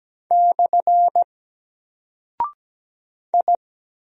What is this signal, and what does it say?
Morse code, heard as a keyed tone, about 17 wpm, 700 Hz. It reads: / I
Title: Single Character (Letter:Number:Punctuation:Prosign) 17wpm